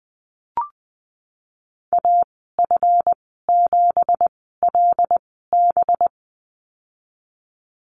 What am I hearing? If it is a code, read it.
Morse code, heard as a keyed tone, about 19 wpm, spaced as for 14 wpm, 700 Hz. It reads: AF7LB
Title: 193 Lesson 31g - Letter v - Call Signs 20wpm